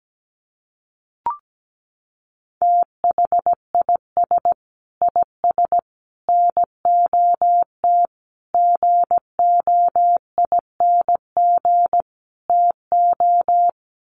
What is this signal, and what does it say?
Morse code, heard as a keyed tone, about 17 wpm, 700 Hz. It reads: THIS IS NOT GOING TO
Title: Sets of 5 Words Encore2 17wpm